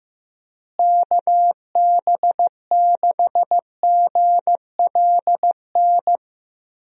Morse code, heard as a keyed tone, about 15 wpm, 700 Hz. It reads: KB6GLN
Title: USA Call Signs - 6 Characters - Repeated 3x - 15wpm